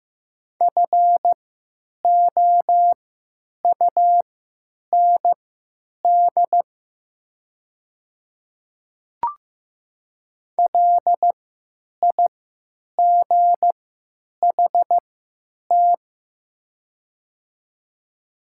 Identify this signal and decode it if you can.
Morse code, heard as a keyed tone, about 15 wpm, spaced as for 9 wpm, 700 Hz. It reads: FOUND LIGHT
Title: 3x Character Spacing - Top 500 Words - 15wpm